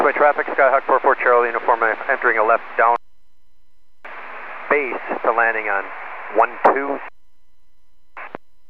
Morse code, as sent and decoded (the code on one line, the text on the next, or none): none